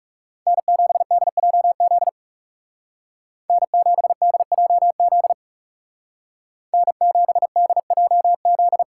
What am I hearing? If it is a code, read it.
Morse code, heard as a keyed tone, about 36 wpm, 700 Hz. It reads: N7DJZ N7DJZ N7DJZ
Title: Speed Racing - USA 1x3 Call Signs 30wpm